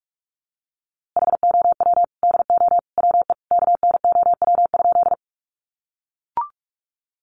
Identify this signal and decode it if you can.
Morse code, heard as a keyed tone, about 44 wpm, 700 Hz. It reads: HOW DO WE KNOW?
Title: A Sound Of Thunder 45wpm